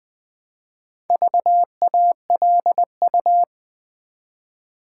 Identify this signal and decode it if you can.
Morse code, heard as a keyed tone, about 20 wpm, 700 Hz. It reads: VALU
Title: Build A Word - 1 Letter At A Time - 3 to 5-Letter Words - 20wpm